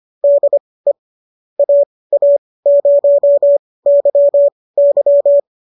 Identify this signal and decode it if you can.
Morse code, heard as a keyed tone, about 25 wpm, spaced as for 18 wpm, 570 Hz. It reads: DE AA0YY